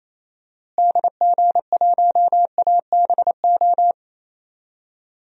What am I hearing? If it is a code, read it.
Morse code, heard as a keyed tone, about 28 wpm, 700 Hz. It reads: DG1ABO
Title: DXCC-100 Call Signs - 6 Characters - Repeated 3x - 28wpm